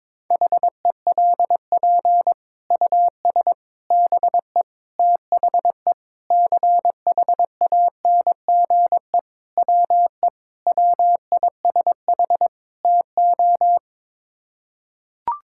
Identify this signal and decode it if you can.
Morse code, heard as a keyed tone, about 22 wpm, 700 Hz. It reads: HELP US BE THE CHANGE WE WISH TO
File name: practice / 22wpm / Sets of 8 Words - 22wpm